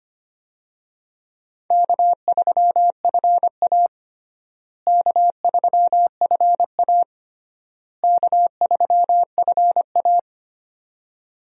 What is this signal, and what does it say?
Morse code, heard as a keyed tone, about 25 wpm, 700 Hz. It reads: K3FA K3FA K3FA
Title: USA Call Signs - 4 Characters - Repeated 3x - 25wpm